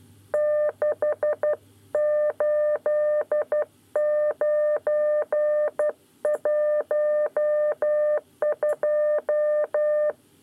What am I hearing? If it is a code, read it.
Morse code, heard as a keyed tone, about 12 wpm, 600 Hz. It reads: 68912
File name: 9 lezione cw 2